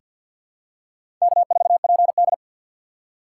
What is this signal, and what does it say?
Morse code, heard as a keyed tone, about 47 wpm, 700 Hz. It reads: K4PD
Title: USA Call Signs - 6x Character Spacing - 50wpm